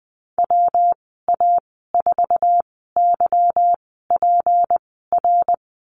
Morse code, heard as a keyed tone, about 20 wpm, spaced as for 15 wpm, 700 Hz. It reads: WA4YPR
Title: 186 Lesson 30g - Letter b - Call Signs 20wpm